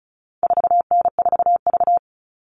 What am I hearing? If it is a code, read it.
Morse code, heard as a keyed tone, about 34 wpm, 700 Hz. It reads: 4N4V
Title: All Cut Numbers - 4-Digit 35wpm